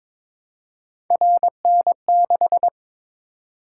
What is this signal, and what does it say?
Morse code, heard as a keyed tone, about 23 wpm, 700 Hz. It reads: RN6
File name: practice / 22wpm / Three Letter-Number Combinations - 22wpm